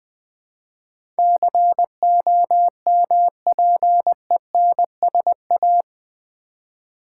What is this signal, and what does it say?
Morse code, heard as a keyed tone, about 19 wpm, 700 Hz. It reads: COMPENSA
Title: Build A Word - 4 Letters At A Time - 12-Letter Words - 20wpm